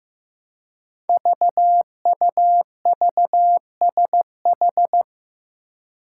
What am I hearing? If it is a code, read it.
Morse code, heard as a keyed tone, about 15 wpm, 700 Hz. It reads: VUVSH
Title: Five Challenging Letter-Number Combinations 15wpm